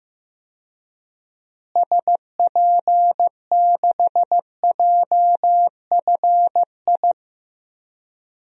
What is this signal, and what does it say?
Morse code, heard as a keyed tone, about 15 wpm, 700 Hz. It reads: SP6JFI
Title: DXCC-100 Call Signs - 6 Characters - Repeated 2x - 15wpm